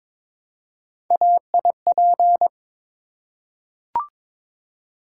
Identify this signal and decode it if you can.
Morse code, heard as a keyed tone, about 22 wpm, 700 Hz. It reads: AIP
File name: practice / 22wpm / Three Letter-Number Combinations - 22wpm